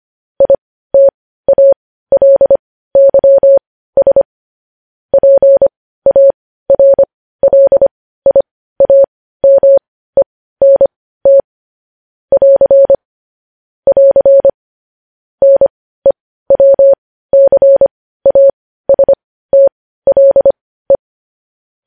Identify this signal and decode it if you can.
Morse code, heard as a keyed tone, about 25 wpm, spaced as for 15 wpm, 550 Hz. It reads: ITALYS PARLIAMENT <AR> <AR> NEWCASTLE
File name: podcast / News headlines-1x-25x15